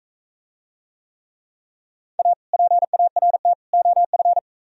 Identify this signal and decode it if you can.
Morse code, heard as a keyed tone, about 44 wpm, 700 Hz. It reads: A PART OF